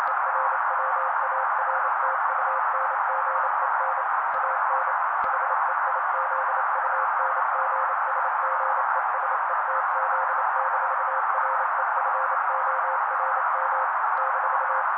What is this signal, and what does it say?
Morse code, heard as a keyed tone, about 28 wpm, 530 Hz. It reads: ERWARTUNGEN AN HEIZUNGSGESETZ-REFORM -
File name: daily / 27wpm noise